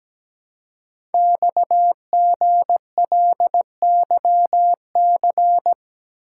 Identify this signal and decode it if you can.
Morse code, heard as a keyed tone, about 17 wpm, 700 Hz. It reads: XGLYC
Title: Code Groups - 5 Letters - 17wpm